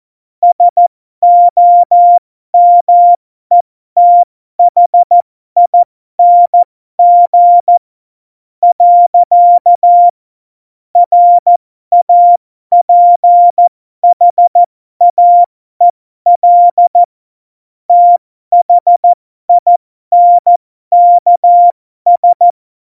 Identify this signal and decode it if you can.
Morse code, heard as a keyed tone, about 14 wpm, spaced as for 12 wpm, 700 Hz. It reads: SOMETHING. RAPHAEL THINKS
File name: Morsekurs / Witze